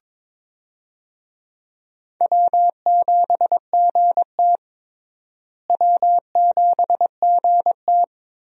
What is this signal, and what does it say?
Morse code, heard as a keyed tone, about 21 wpm, 700 Hz. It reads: W7GT W7GT
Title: USA Call Signs - 4 Characters - Repeated 3x - 22wpm